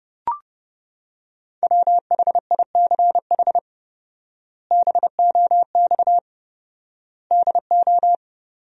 Morse code, heard as a keyed tone, about 30 wpm, 700 Hz. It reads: WHICH BOX DO
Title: 4x Word-Spacing - Sentences from Top 500 Words - 30wpm